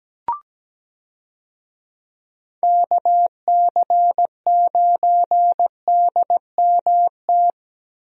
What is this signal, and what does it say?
Morse code, heard as a keyed tone, about 17 wpm, 700 Hz. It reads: KC9DMT
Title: USA Call Signs - 6 Characters - Repeated 2x - 17wpm